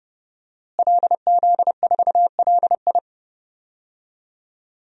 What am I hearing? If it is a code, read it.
Morse code, heard as a keyed tone, about 31 wpm, 700 Hz. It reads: LZ4LI
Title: DXCC-100 Call Signs - 5 Characters - Repeated 3x - 30wpm